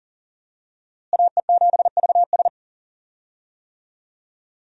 Morse code, heard as a keyed tone, about 40 wpm, 700 Hz. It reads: AE7VS
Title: USA Call Signs - 5 Characters - Repeated 2x - 40wpm